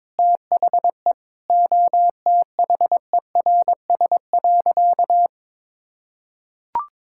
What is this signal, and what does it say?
Morse code, heard as a keyed tone, about 22 wpm, 700 Hz. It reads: THE OTHERS.
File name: practice / 22wpm / Sentences with 8-Word Phrases - 22wpm